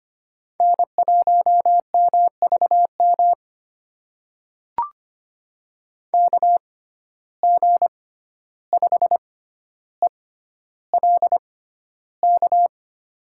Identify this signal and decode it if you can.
Morse code, heard as a keyed tone, about 25 wpm, spaced as for 12 wpm, 700 Hz. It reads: N1MVM KG5ELK
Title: USA Call Signs - 6x Character Spacing - 25wpm